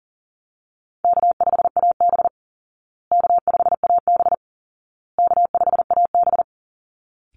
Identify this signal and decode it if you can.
Morse code, heard as a keyed tone, about 40 wpm, 700 Hz. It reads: K5AB K5AB K5AB
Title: Easy Call Signs Repeated 3x - 40wpm